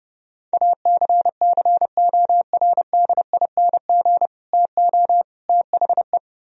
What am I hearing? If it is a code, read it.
Morse code, heard as a keyed tone, about 30 wpm, 700 Hz. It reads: ACCORDING TO THE